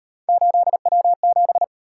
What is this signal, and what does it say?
Morse code, heard as a keyed tone, about 38 wpm, 700 Hz. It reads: 8W7